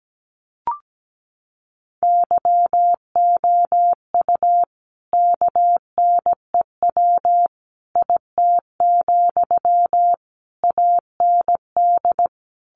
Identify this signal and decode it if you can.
Morse code, heard as a keyed tone, about 17 wpm, 700 Hz. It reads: YOU KNEW IT, AND
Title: Sentences from Top 500 Words 17wpm